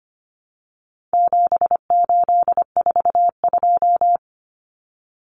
Morse code, heard as a keyed tone, about 25 wpm, 700 Hz. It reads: 7842
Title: Numbers - 4-Digit 25wpm